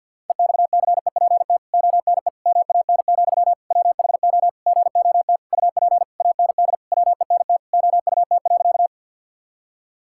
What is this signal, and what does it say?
Morse code, heard as a keyed tone, about 48 wpm, 700 Hz. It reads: EXCEPT ONE MAN, WHO GOT UP AND WENT OUT.
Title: Sentences from Top 1600 Words 50wpm